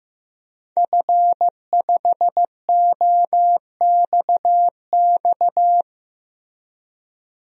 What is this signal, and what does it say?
Morse code, heard as a keyed tone, about 15 wpm, 700 Hz. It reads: F5OXX
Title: DXCC-100 Call Signs - 5 Characters - Repeated 2x - 15wpm